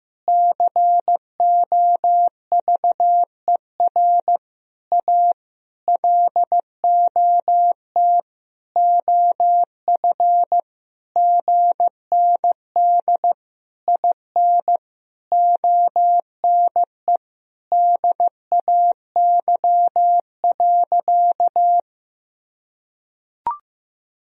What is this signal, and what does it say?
Morse code, heard as a keyed tone, about 15 wpm, 700 Hz. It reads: COVER A LOT OF GND IN ONE DAY.
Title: Sentences from Top 500 Words and Abbreviations 15wpm